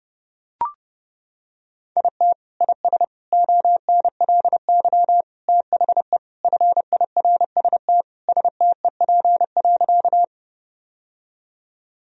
Morse code, heard as a keyed tone, about 32 wpm, 700 Hz. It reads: IT IS ONLY THE FIRST STEP.
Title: Sentences from Top 1000 Words - Review Part 2 - 30wpm